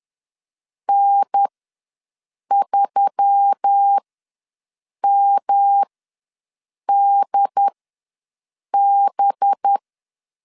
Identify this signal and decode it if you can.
Morse code, heard as a keyed tone, about 10 wpm, spaced as for 6 wpm, 800 Hz. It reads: N3MDB
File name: LEZIONI / 6 PKX0 / 6 PKX ws 10wpm 3706